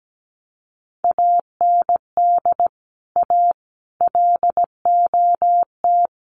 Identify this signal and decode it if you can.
Morse code, heard as a keyed tone, about 17 wpm, 700 Hz. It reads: AND A LOT
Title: Sets of 4 Words 17wpm